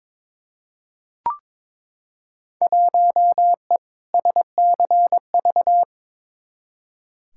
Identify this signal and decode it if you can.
Morse code, heard as a keyed tone, about 22 wpm, 700 Hz. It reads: 1E SCV
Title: Field Day Exchanges - 22wpm